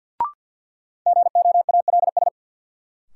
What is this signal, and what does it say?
Morse code, heard as a keyed tone, about 52 wpm, 700 Hz. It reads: GOALS
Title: Sentences from Top 1000 Words - Review Part 6 - New Word Drill - 50wpm